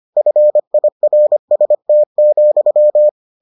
Morse code, heard as a keyed tone, about 25 wpm, 600 Hz. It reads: FIRST,